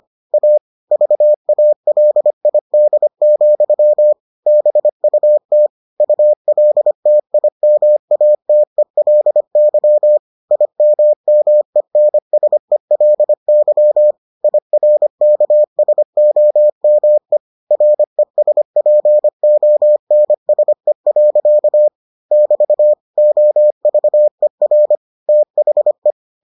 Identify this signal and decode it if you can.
Morse code, heard as a keyed tone, about 25 wpm, 600 Hz. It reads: A VALID, BUT ULTIMATELY IMMENSELY IRKSOME RESPONSE. <BT> OVER THE